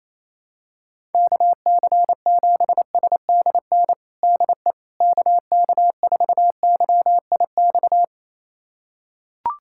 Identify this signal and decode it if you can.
Morse code, heard as a keyed tone, about 28 wpm, 700 Hz. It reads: KC7SDN DE KK4YIX
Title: USA Callsign DE Callsign - 6 Characters - 28wpm